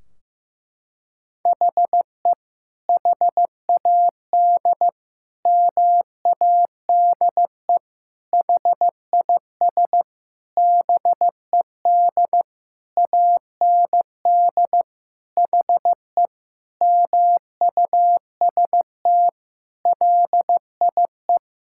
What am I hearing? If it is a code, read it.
Morse code, heard as a keyed tone, about 15 wpm, 700 Hz. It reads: HE HAD MADE HIS BED AND HE MUST LIE
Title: Sentences from Top 1600 Words 15wpm